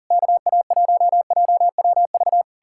Morse code, heard as a keyed tone, about 56 wpm, spaced as for 48 wpm, 700 Hz. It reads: KA1JWV
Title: Speed-Racing Common QSO Phrases - 40wpm